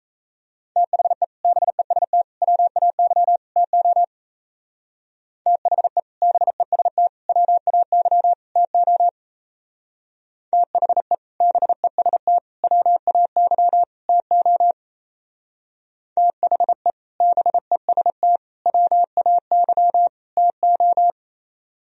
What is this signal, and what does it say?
Morse code, heard as a keyed tone, about 34 wpm, 700 Hz. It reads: THE BEST WAY TO THE BEST WAY TO THE BEST WAY TO THE BEST WAY TO
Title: Speed-Racing - Sets of 4 Words 28wpm